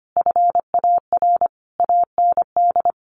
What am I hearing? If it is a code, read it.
Morse code, heard as a keyed tone, about 25 wpm, 700 Hz. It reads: FAR AND
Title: Sentences from Top 1200 Words 25wpm